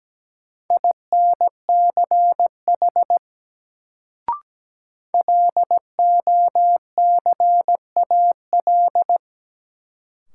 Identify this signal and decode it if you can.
Morse code, heard as a keyed tone, about 17 wpm, 700 Hz. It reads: INCH LOCAL
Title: Sentences from Top 1000 Words - Review Part 3 - New Word Drill - 17wpm